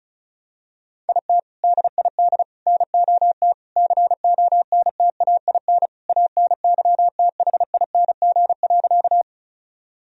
Morse code, heard as a keyed tone, about 36 wpm, 700 Hz. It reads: IT DID NOT CONTAIN ANYTHING.
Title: Sentences from Top 1000 Words - Review Part 2 - 35wpm